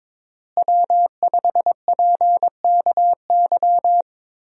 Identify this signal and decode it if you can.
Morse code, heard as a keyed tone, about 22 wpm, 700 Hz. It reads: W5PKY
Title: USA Call Signs - 5 Characters - Repeated 3x - 22wpm